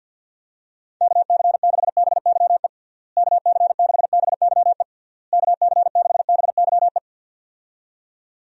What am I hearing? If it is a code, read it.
Morse code, heard as a keyed tone, about 52 wpm, 700 Hz. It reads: KC6BYE KC6BYE KC6BYE
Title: USA Call Signs - 6 Characters - Repeated 3x - 50wpm